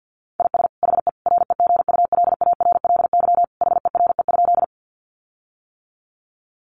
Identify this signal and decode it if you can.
Morse code, heard as a keyed tone, about 50 wpm, 700 Hz. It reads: IS HE REGULARLY HERE?
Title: Sentences from Top 1400 Words 50wpm